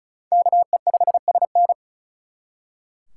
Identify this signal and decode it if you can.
Morse code, heard as a keyed tone, about 44 wpm, 700 Hz. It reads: KE5SN